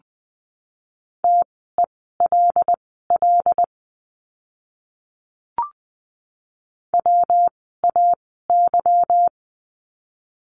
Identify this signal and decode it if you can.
Morse code, heard as a keyed tone, about 20 wpm, spaced as for 15 wpm, 700 Hz. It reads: TELL WAY
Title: 230 Lesson 37f - Letter x - Words 20wpm